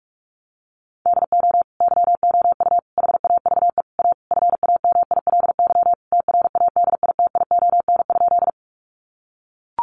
Morse code, heard as a keyed tone, about 44 wpm, 700 Hz. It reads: DO YOU HAVE A FAMILY TRADITION?
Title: Sentences from Top 1500 Words 45wpm